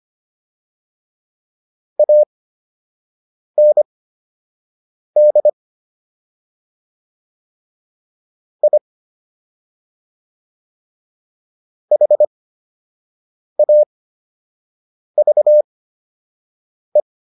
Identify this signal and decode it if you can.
Morse code, heard as a keyed tone, about 25 wpm, spaced as for 6 wpm, 600 Hz. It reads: AND I HAVE